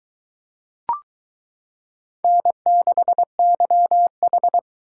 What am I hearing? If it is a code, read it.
Morse code, heard as a keyed tone, about 23 wpm, 700 Hz. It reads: N6YH